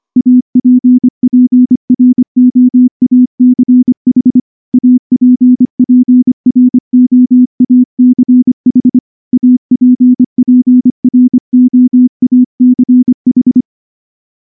Morse code, heard as a keyed tone, about 24 wpm, 250 Hz. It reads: APPROACH APPROACH APPROACH
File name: morse / morse-training-25WPM